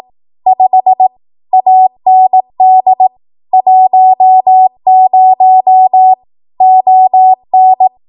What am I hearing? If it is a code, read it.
Morse code, heard as a keyed tone, about 18 wpm, 750 Hz. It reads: 5 AND 10 ON
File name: Archive / 18 WPM / 18WPM